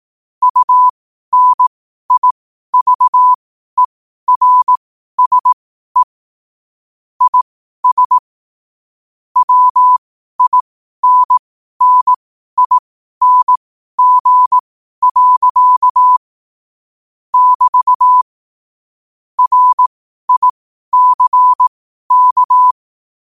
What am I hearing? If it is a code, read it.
Morse code, heard as a keyed tone, about 18 wpm, spaced as for 12 wpm, 1000 Hz. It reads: UNIVERSE IS WINNING. <BT> RICK
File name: qotdfiles / 12wpm